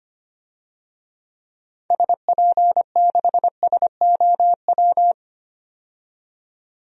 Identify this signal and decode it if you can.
Morse code, heard as a keyed tone, about 25 wpm, 700 Hz. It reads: SP6SOW
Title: DXCC-100 Call Signs - 6 Characters - 25wpm